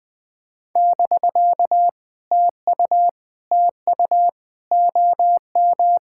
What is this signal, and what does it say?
Morse code, heard as a keyed tone, about 21 wpm, 700 Hz. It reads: <BK> TU TU OM
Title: POTA QSO Practice - 20wpm